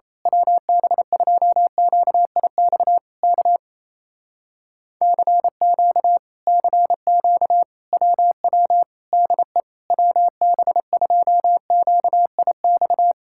Morse code, heard as a keyed tone, about 27 wpm, 700 Hz. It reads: WB2QIX K CQ CQ WW DE WB2QIX
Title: Speed-Racing Common QSO Phrases Encore - 28wpm